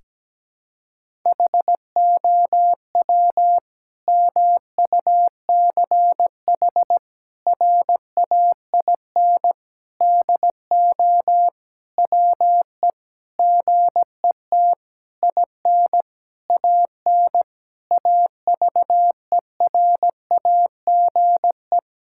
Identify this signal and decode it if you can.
Morse code, heard as a keyed tone, about 17 wpm, 700 Hz. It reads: HOW MUCH RAIN DO WE GET IN AN AVERAGE
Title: Sentences from Top 1000 Words - Review Part 6 - 17wpm